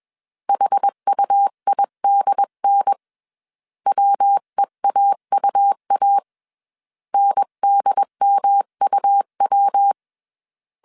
Morse code, heard as a keyed tone, about 21 wpm, 800 Hz. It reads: HUIDN WEAUA NDMUW